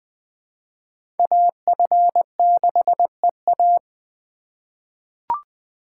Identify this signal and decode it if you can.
Morse code, heard as a keyed tone, about 20 wpm, 700 Hz. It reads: AF6EA
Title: USA Call Signs - 5 Characters - 2x Character Spacing - 20wpm